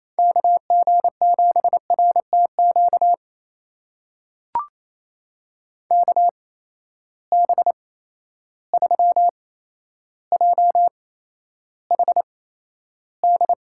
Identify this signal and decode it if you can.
Morse code, heard as a keyed tone, about 28 wpm, spaced as for 12 wpm, 700 Hz. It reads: KG7RTQ KB3JHD
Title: USA Call Signs - 6 Characters - 8x Character Spacing - 28wpm